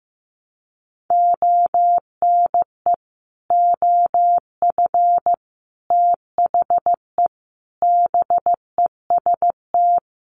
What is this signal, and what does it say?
Morse code, heard as a keyed tone, about 15 wpm, 700 Hz. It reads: ONE OF THE BEST
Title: Sets of 4 Words 15wpm